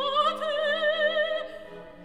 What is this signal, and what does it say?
There is no code.